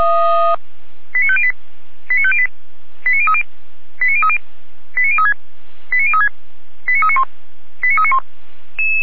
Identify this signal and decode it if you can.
Five-tone selective call (ZVEI), heard as quick runs of five tones, tones of about 70 ms. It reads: 89508 89508 89030 89030 89036 89036 89391 89391